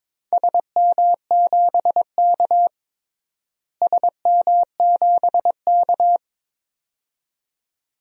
Morse code, heard as a keyed tone, about 21 wpm, 700 Hz. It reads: SM7K SM7K
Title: DXCC-100 Call Signs - 4 Characters - Repeated 2x - 22wpm